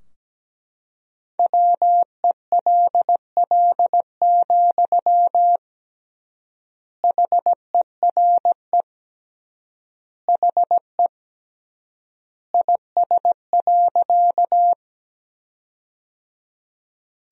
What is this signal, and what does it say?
Morse code, heard as a keyed tone, about 17 wpm, 700 Hz. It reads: WELL, HERE HE IS.
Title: 3x Word-Spacing - Sentences from Top 200 Words - 17wpm